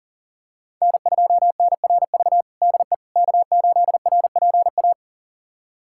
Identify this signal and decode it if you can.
Morse code, heard as a keyed tone, about 40 wpm, 700 Hz. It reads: N2NRV DE K8RPA
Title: USA Callsign DE Callsign - 5 Characters - 40wpm